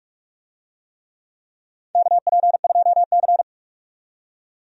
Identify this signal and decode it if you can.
Morse code, heard as a keyed tone, about 46 wpm, 700 Hz. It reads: KP2C
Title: DXCC-100 Call Signs - 4 Characters - 45wpm